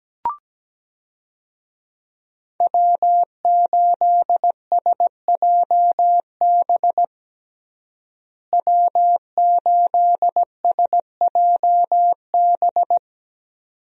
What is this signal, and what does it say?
Morse code, heard as a keyed tone, about 17 wpm, 700 Hz. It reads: W8SJB W8SJB
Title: USA Call Signs - 5 Characters - Repeated 3x - 17wpm